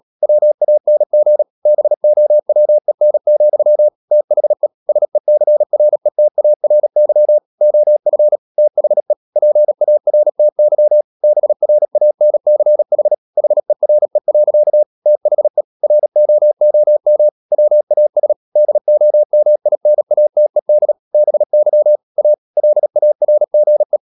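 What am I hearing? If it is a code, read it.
Morse code, heard as a keyed tone, about 37 wpm, 600 Hz. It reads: WANG BOWEN, THE SECRETARY OF THE PARTY BRANCH HERE. THE ROOM WAS DOMINATED BY A LARGE
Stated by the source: Morse code practice at 35–39 wpm sending English text of Ken Follett's novel Never, from CW med Gnister.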